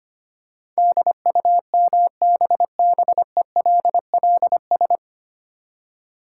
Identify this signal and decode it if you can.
Morse code, heard as a keyed tone, about 25 wpm, 700 Hz. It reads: DUMBBELLS
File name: practice / 25wpm / Easy Words with Hints - 25wpm